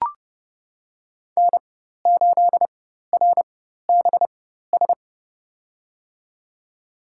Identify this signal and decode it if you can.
Morse code, heard as a keyed tone, about 29 wpm, spaced as for 14 wpm, 700 Hz. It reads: N8RBS
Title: USA Call Signs - 4x Character Spacing - 30wpm